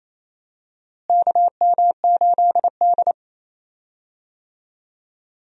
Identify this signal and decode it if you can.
Morse code, heard as a keyed tone, about 28 wpm, 700 Hz. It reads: KM8D
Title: USA Call Signs - 4 Characters - Repeated 2x - 28wpm